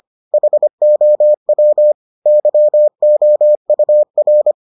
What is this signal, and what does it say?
Morse code, heard as a keyed tone, about 25 wpm, 600 Hz. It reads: HOW YOUR